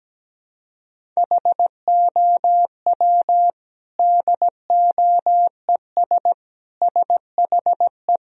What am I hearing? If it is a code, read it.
Morse code, heard as a keyed tone, about 17 wpm, 700 Hz. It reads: HOW DOES SHE
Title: Sentences from Top 1000 Words - Review Part 1 - 17wpm